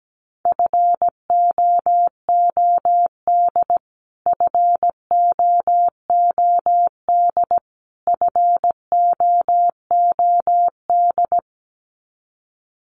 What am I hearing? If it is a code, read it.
Morse code, heard as a keyed tone, about 17 wpm, 700 Hz. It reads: FOOD FOOD FOOD
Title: Top 500 Words Repeated 17wpm